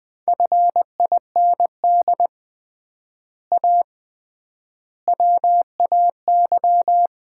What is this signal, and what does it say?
Morse code, heard as a keyed tone, about 20 wpm, 700 Hz. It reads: FIND A WAY
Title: 3x Word-Spacing - Sets of 4 Words - 20wpm